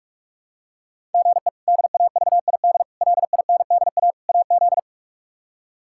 Morse code, heard as a keyed tone, about 46 wpm, 700 Hz. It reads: GE DAVID LINDA AZ
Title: K1USN SST QSO Practice - 45wpm